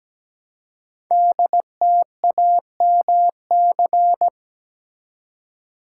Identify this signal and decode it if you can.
Morse code, heard as a keyed tone, about 17 wpm, 700 Hz. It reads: DTAMC
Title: Code Groups - 5 Letters - 17wpm